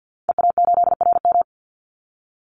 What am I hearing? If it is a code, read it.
Morse code, heard as a keyed tone, about 50 wpm, 700 Hz. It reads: EA8RM
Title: Call Signs Repeated 2x - 50wpm